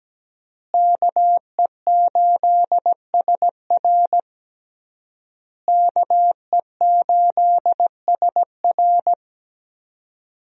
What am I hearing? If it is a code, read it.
Morse code, heard as a keyed tone, about 17 wpm, 700 Hz. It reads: KE8SR KE8SR
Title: USA Call Signs - 5 Characters - Repeated 3x - 17wpm